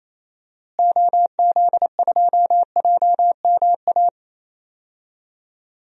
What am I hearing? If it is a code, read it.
Morse code, heard as a keyed tone, about 28 wpm, 700 Hz. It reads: OZ2JMA
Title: DXCC-100 Call Signs - 6 Characters - Repeated 3x - 28wpm